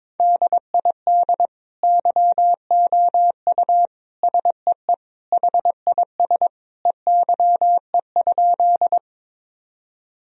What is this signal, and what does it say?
Morse code, heard as a keyed tone, about 21 wpm, 700 Hz. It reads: DID YOU SEE HIS EYE?
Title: Sentences from Top 1000 Words - Review Part 1 - 22wpm